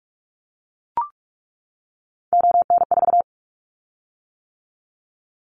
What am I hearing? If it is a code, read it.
Morse code, heard as a keyed tone, about 44 wpm, 700 Hz. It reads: ON4
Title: Callsign Prefixes for Top 100 DXCC - 45wpm